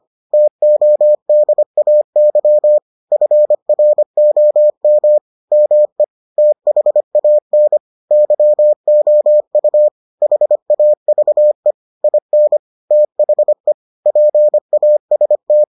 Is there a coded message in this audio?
Yes: Morse code from 0.3 to 15.6 s, heard as a keyed tone, about 25 wpm, 600 Hz. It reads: TODAY FROM ME THAN YOU HAVE IN THE PAST